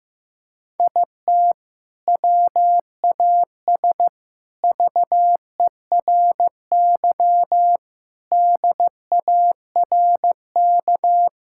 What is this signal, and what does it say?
Morse code, heard as a keyed tone, about 16 wpm, 700 Hz. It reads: IT WAS VERY DARK
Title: Sentences from Top 1000 Words - Review Part 6 - 15wpm